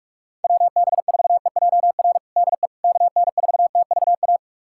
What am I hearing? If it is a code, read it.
Morse code, heard as a keyed tone, about 46 wpm, 700 Hz. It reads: WB4EJR DE KN4TVA
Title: USA Callsign DE Callsign 45wpm